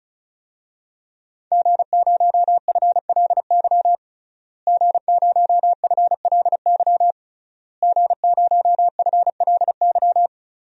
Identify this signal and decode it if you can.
Morse code, heard as a keyed tone, about 36 wpm, 700 Hz. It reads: G0FLY G0FLY G0FLY
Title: DXCC-100 Call Signs - 5 Characters - Repeated 3x - 35wpm